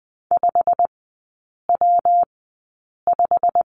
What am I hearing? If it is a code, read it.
Morse code, heard as a keyed tone, about 20 wpm, spaced as for 8 wpm, 700 Hz. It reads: 5W5
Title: 114 Lesson 19e - Letter w - Three Characters 20wpm